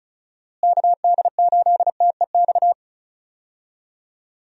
Morse code, heard as a keyed tone, about 36 wpm, 700 Hz. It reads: KD8TEX
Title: USA Call Signs - 6 Characters - Repeated 3x - 35wpm